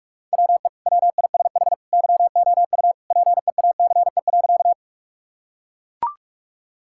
Morse code, heard as a keyed tone, about 41 wpm, 700 Hz. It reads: WE WISH YOU PEACE.
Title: Sentences from Top 1600 Words 45wpm